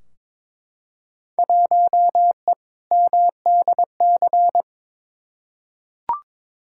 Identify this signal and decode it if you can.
Morse code, heard as a keyed tone, about 22 wpm, 700 Hz. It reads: 1E MDC
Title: Field Day Exchanges Encore - 22wpm